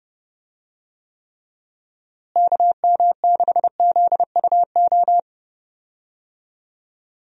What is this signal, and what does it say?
Morse code, heard as a keyed tone, about 30 wpm, 700 Hz. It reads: KM6ZUO
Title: USA Call Signs - 6 Characters - Repeated 2x - 30wpm